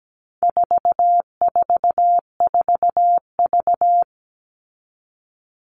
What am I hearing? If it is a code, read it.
Morse code, heard as a keyed tone, about 17 wpm, 700 Hz. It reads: 444V
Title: All Cut Numbers - 4-Digit 17wpm